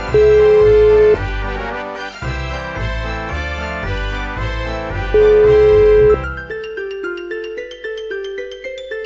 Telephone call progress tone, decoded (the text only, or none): ringback tone